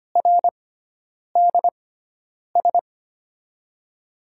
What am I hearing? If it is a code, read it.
Morse code, heard as a keyed tone, about 25 wpm, spaced as for 9 wpm, 700 Hz. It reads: RDS
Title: USA Call Signs - 6x Character Spacing - 25wpm